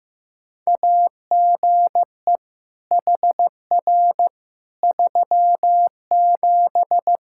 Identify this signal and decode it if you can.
Morse code, heard as a keyed tone, about 15 wpm, 700 Hz. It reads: AGE HR 37
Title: Practice QSO - Exchange 3 - More Info - 15wpm